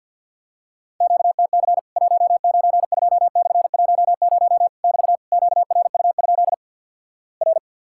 Morse code, heard as a keyed tone, about 48 wpm, 700 Hz. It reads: QTC 192/10 <BT> QRU? R
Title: Worked All Europe DX Contest - QTC Exchange - 50wpm